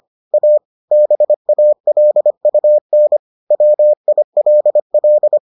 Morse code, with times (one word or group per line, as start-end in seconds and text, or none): 0.3-0.6 A
0.9-3.2 BALUN
3.5-5.4 WILL